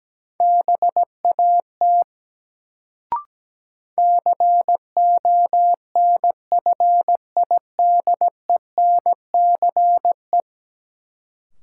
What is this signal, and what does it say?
Morse code, heard as a keyed tone, about 17 wpm, 700 Hz. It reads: BAT CONFIDENCE
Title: Sentences from Top 1500 Words - New Word Drill - 17wpm